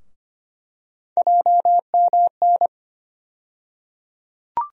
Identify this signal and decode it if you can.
Morse code, heard as a keyed tone, about 25 wpm, 700 Hz. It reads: JMN
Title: USA Call Sign Postfixes - 3 Letters 25wpm